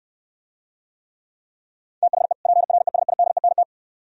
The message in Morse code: THE PRESIDENT

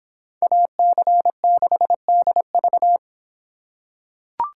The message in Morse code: AC6DV

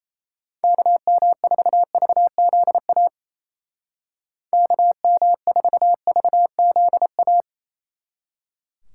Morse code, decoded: KM4VZA KM4VZA

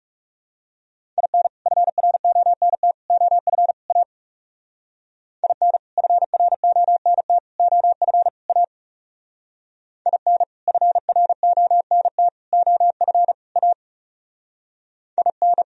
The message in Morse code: IN FRONT OF A IN FRONT OF A IN FRONT OF A IN

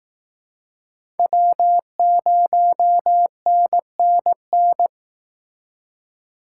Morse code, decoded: W0NNN

